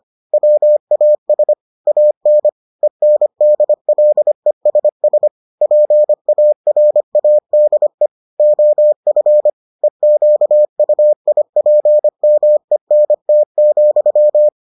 Morse code, heard as a keyed tone, about 25 wpm, 600 Hz. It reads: WAS AN ENDLESS PARADE OF EQUIPMENT,